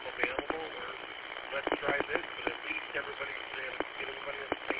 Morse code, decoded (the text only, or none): none